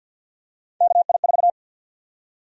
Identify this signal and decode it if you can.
Morse code, heard as a keyed tone, about 52 wpm, 700 Hz. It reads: KI4
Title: USA Call Sign Prefixes - 50wpm